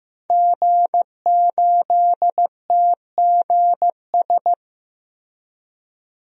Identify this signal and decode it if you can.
Morse code, heard as a keyed tone, about 16 wpm, 700 Hz. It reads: G8TGS
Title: DXCC-100 Call Signs - 5 Characters - Repeated 2x - 15wpm